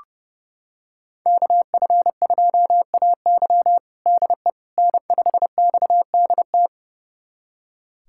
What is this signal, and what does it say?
Morse code, heard as a keyed tone, about 32 wpm, 700 Hz. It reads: KF2AY DE N5XDT
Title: USA Callsign DE Callsign - 5 Characters - 30wpm